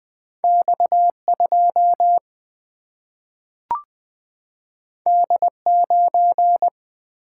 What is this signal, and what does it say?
Morse code, heard as a keyed tone, about 20 wpm, 700 Hz. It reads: X 2 D 9
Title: Single-Letter Single-Number 20wpm